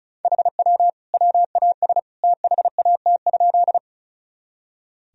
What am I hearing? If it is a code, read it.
Morse code, heard as a keyed tone, about 36 wpm, 700 Hz. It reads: HW WAS THAT?
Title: Sentences from Top 100 Words and Abbreviations 35wpm